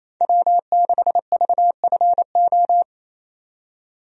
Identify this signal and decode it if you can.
Morse code, heard as a keyed tone, about 27 wpm, 700 Hz. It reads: W6VFO
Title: USA Call Signs - 5 Characters - 28wpm